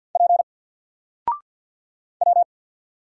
Morse code, heard as a keyed tone, about 48 wpm, 700 Hz. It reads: P W
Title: Single Letter-Number Alt 50wpm